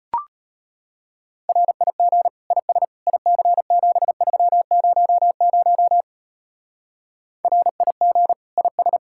Morse code, heard as a keyed tone, about 29 wpm, 700 Hz. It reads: RIG IS IC7300 RIG IS
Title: Speed-Racing Common QSO Phrases Encore - 25wpm